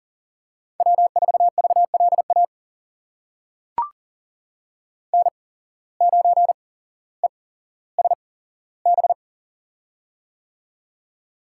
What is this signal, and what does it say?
Morse code, heard as a keyed tone, about 40 wpm, spaced as for 17 wpm, 700 Hz. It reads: W4VLA N9ESB
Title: USA Call Signs - 5 Characters - 8x Character Spacing - 40wpm